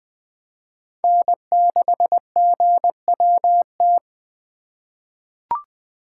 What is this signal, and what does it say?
Morse code, heard as a keyed tone, about 20 wpm, 700 Hz. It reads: N6GWT
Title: USA Call Signs - 5 Characters - 3x Character Spacing - 20wpm